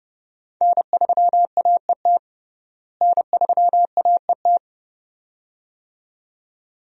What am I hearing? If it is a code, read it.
Morse code, heard as a keyed tone, about 32 wpm, 700 Hz. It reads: N3AET N3AET
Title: USA Call Signs - 5 Characters - Repeated 2x - 30wpm